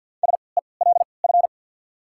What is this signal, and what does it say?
Morse code, heard as a keyed tone, about 53 wpm, spaced as for 29 wpm, 690 Hz. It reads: SELF
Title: 3x Character Spacing - Top 500 Words - 50wpm